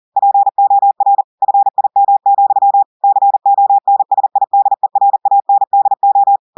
Morse code, heard as a keyed tone, about 40 wpm, 800 Hz. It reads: POR FIM, CONSIDERANDO
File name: audios / portugais / portugais texte en clair 2 vitesse 40 mots minutes